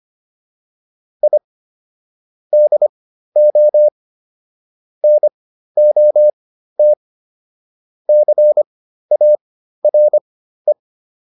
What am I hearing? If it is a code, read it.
Morse code, heard as a keyed tone, about 25 wpm, spaced as for 13 wpm, 600 Hz. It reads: I DO NOT CARE
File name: WD302 13